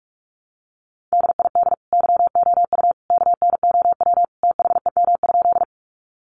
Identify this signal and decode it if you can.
Morse code, heard as a keyed tone, about 44 wpm, 700 Hz. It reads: DID YOU KNOW THEM?
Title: Sentences from Top 100 Words 45wpm